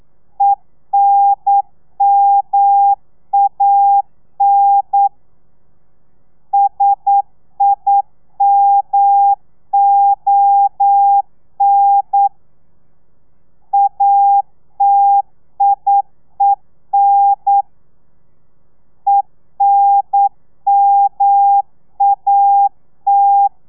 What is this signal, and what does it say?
Morse code, heard as a keyed tone, about 9 wpm, 800 Hz. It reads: ENMAN SIMON ATIEN ENMAT
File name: telegrafia / i2tfj / Lez2 40